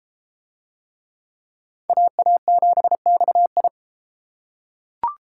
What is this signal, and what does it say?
Morse code, heard as a keyed tone, about 33 wpm, 700 Hz. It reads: AA7XI